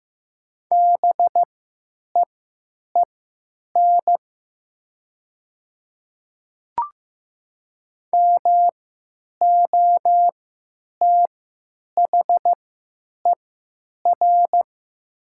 Morse code, heard as a keyed tone, about 15 wpm, spaced as for 9 wpm, 700 Hz. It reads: BEEN MOTHER
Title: 3x Character Spacing - Top 200 Words - 15wpm